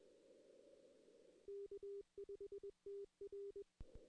Morse code, heard as a keyed tone, about 21 wpm, 410 Hz. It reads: K5TR
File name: 2016 arrl dx cw / 2016-ARRL-DX-CW-K5TR- - 0-3